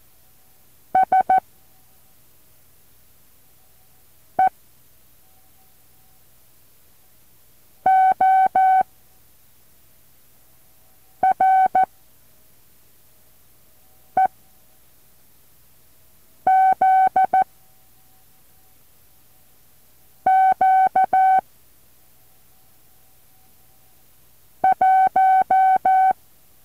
Morse code, heard as a keyed tone, about 14 wpm, 760 Hz. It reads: S E O R E Z Q 1
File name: intl morse code2